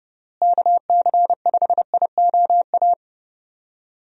Morse code, heard as a keyed tone, about 30 wpm, 700 Hz. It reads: KC5IOA
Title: USA Call Signs - 6 Characters - Repeated 3x - 30wpm